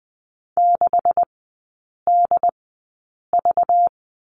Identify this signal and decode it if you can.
Morse code, heard as a keyed tone, about 20 wpm, spaced as for 8 wpm, 700 Hz. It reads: 6DV
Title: 191 Lesson 31e - Letter v - Three Characters 20wpm